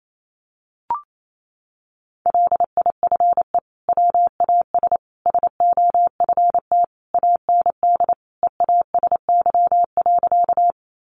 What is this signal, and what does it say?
Morse code, heard as a keyed tone, about 28 wpm, 700 Hz. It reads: LIFE WAS SOFT AND EASY.
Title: Sentences from Top 1500 Words 28wpm